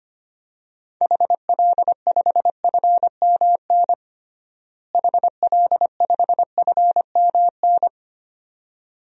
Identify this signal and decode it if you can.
Morse code, heard as a keyed tone, about 25 wpm, 700 Hz. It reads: HL5FMN HL5FMN
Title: DXCC-100 Call Signs - 6 Characters - Repeated 2x - 25wpm